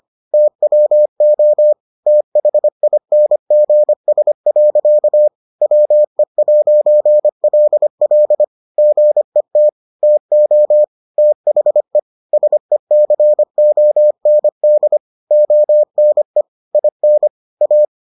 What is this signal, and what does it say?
Morse code, heard as a keyed tone, about 25 wpm, 600 Hz. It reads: TWO THINGS. WE'LL GET TO THE SECOND ONE IN A